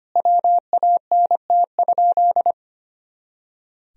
Morse code, heard as a keyed tone, about 25 wpm, 700 Hz. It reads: WANT?
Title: Sentences from Top 200 Words and Abbreviations 25wpm